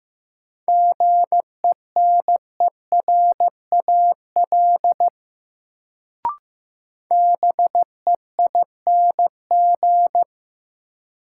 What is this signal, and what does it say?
Morse code, heard as a keyed tone, about 15 wpm, 700 Hz. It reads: GENERAL BEING
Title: Sentences from Top 600 Words - New Word Drill - 15wpm